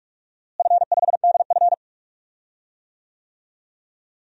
Morse code, heard as a keyed tone, about 46 wpm, 700 Hz. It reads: F5DF
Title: DXCC-100 Call Signs - 4 Characters - Repeated 3x - 45wpm